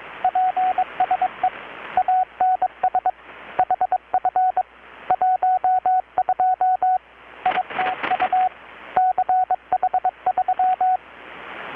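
Morse code, heard as a keyed tone, about 23 wpm, 740 Hz, with radio noise. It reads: PSE ANS HF 12 ITU CH3
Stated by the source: Morse from KPH's Night of Nights XX maritime radiotelegraphy event, received in Chiapas, Mexico about 3,800 km away on a Kenwood R-600 with a grounded 10 m long-wire antenna.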